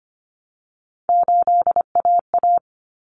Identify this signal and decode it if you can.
Morse code, heard as a keyed tone, about 25 wpm, 700 Hz. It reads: 8AA
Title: Common Cut Numbers - 3-Digit 25wpm